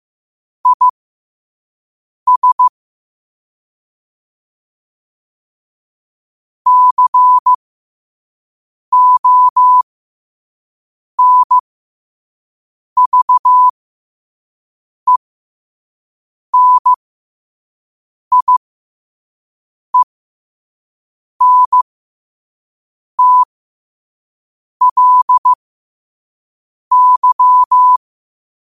Morse code, heard as a keyed tone, about 15 wpm, spaced as for 5 wpm, 1000 Hz. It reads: IS CONVENIENTLY
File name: qotdfiles / 5wpm